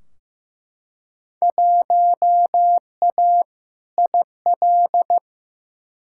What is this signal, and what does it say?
Morse code, heard as a keyed tone, about 15 wpm, 700 Hz. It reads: 1A IL
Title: Field Day Exchanges Encore - 15wpm